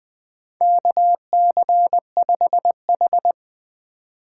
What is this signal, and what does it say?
Morse code, heard as a keyed tone, about 21 wpm, 700 Hz. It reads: KC5H
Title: 2023 POTA Hunter Call Signs - 20wpm